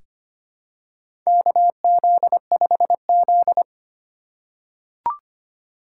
Morse code, heard as a keyed tone, about 25 wpm, 700 Hz. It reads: KZ5Z
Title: K1USN SST QSO Practice - 25wpm